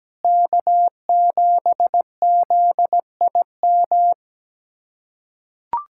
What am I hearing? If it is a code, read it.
Morse code, heard as a keyed tone, about 17 wpm, 700 Hz. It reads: K7ZIM
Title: USA Callsign DE Callsign - 5 Characters - 17wpm